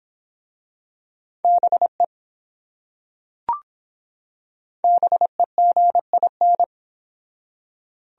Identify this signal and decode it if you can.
Morse code, heard as a keyed tone, about 24 wpm, 700 Hz. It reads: BE BEGIN